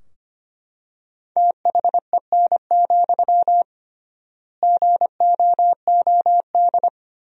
Morse code, heard as a keyed tone, about 25 wpm, 700 Hz. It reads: THEN, GOOD
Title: 3x Word-Spacing - Sentences from Top 300 Words - 25wpm